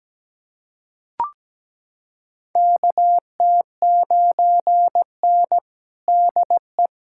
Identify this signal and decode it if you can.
Morse code, heard as a keyed tone, about 17 wpm, 700 Hz. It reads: KT9N DE
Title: USA Callsign DE Callsign - 4 Characters - 17wpm